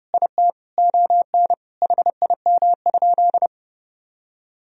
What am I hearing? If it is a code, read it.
Morse code, heard as a keyed tone, about 30 wpm, 700 Hz. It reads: IT ON HIM?
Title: Sentences from Top 1000 Words - Review Part 1 - 30wpm